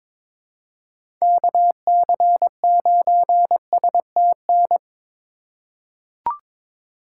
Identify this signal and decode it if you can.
Morse code, heard as a keyed tone, about 22 wpm, 700 Hz. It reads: KC9STN
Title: USA Call Signs - 6 Characters - 4x Character Spacing - 22wpm